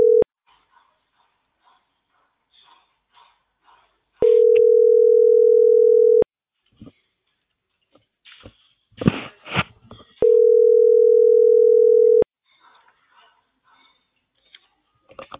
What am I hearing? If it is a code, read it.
Ringback tone (North America), 440 + 480 Hz.